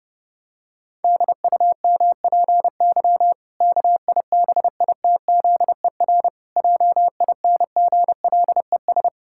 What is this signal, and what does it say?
Morse code, heard as a keyed tone, about 32 wpm, 700 Hz. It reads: DUMPY KIBITZER JINGLES
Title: Pangram Challenge v2 - 30wpm